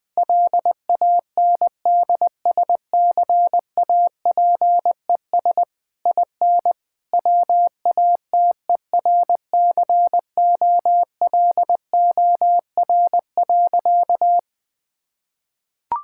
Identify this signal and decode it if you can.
Morse code, heard as a keyed tone, about 21 wpm, 700 Hz. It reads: LANDSCAPES IN WATERCOLOR.
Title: Difficult Sentences with Hints - 20wpm